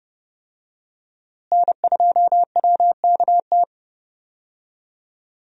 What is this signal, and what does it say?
Morse code, heard as a keyed tone, about 32 wpm, 700 Hz. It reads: N2WKT
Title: USA Call Signs - 5 Characters - Repeated 3x - 30wpm